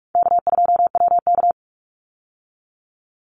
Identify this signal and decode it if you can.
Morse code, heard as a keyed tone, about 44 wpm, 700 Hz. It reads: K2WK
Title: Easy Call Signs Repeated 3x - 45wpm